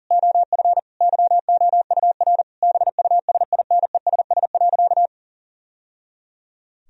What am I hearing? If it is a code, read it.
Morse code, heard as a keyed tone, about 40 wpm, 700 Hz. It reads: OF YOUR BUSINESS.
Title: Sentences from Top 1000 Words - Review Part 4 - 40wpm